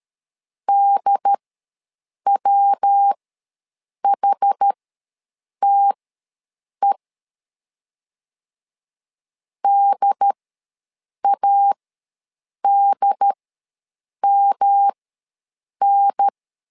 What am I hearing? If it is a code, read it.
Morse code, heard as a keyed tone, about 13 wpm, spaced as for 7 wpm, 800 Hz. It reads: DWHTE DADMN
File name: LEZIONI / 3 MWRO / 3 MW ws 12wpm 1816